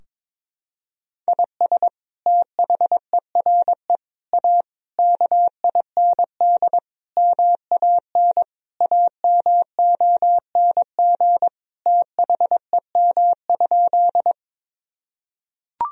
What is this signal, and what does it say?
Morse code, heard as a keyed tone, about 21 wpm, 700 Hz. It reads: IS THERE A KIND MAN AMONG THEM?
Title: Sentences from Top 1000 Words - Review Part 3 - 22wpm